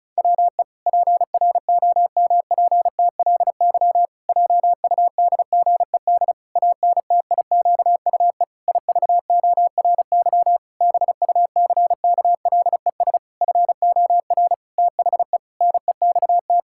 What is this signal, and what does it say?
Morse code, heard as a keyed tone, about 36 wpm, 700 Hz. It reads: WE PROMPTLY JUDGED ANTIQUE IVORY BUCKLES FOR THE NEXT
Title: Pangram Challenge v2 - 35wpm